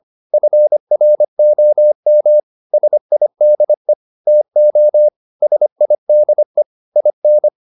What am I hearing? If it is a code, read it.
Morse code, heard as a keyed tone, about 25 wpm, 600 Hz. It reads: FROM SIDE TO SIDE IN